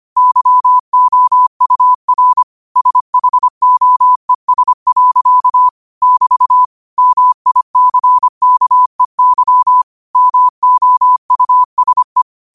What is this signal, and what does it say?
Morse code, heard as a keyed tone, about 25 wpm, 1000 Hz. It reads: YOUR SHOES. <BT> MICKEY MOUSE
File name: qotdfiles / 25wpm